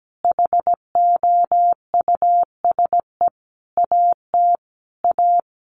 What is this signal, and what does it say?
Morse code, heard as a keyed tone, about 17 wpm, 700 Hz. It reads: HOUSE AT A
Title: Sentences from Top 1200 Words 17wpm